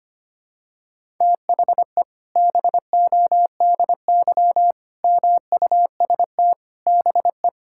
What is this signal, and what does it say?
Morse code, heard as a keyed tone, about 25 wpm, 700 Hz. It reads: THE BODY MUST BE
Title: Sentences from Top 1000 Words - Review Part 5 - 25wpm